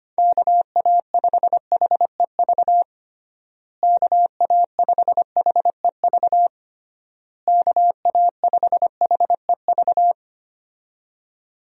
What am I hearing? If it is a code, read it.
Morse code, heard as a keyed tone, about 25 wpm, 700 Hz. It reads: KA5HEV KA5HEV KA5HEV
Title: USA Call Signs - 6 Characters - Repeated 3x - 25wpm